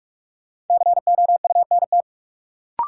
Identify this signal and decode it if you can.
Morse code, heard as a keyed tone, about 46 wpm, 700 Hz. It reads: COUNT
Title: Sentences from Top 1500 Words - New Word Drill Alt - 45wpm